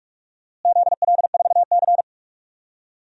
Morse code, heard as a keyed tone, about 47 wpm, 700 Hz. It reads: ZL4C
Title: DXCC-100 Call Signs - 4 Characters - Repeated 2x - 45wpm